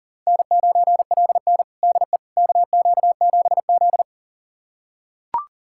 Morse code, heard as a keyed tone, about 40 wpm, 700 Hz. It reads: N9LN DE KQ7Z
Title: USA Callsign DE Callsign - 4 Characters - 40wpm